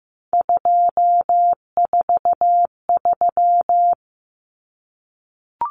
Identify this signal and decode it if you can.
Morse code, heard as a keyed tone, about 15 wpm, 700 Hz. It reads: 243
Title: Numbers - 3-Digit 15wpm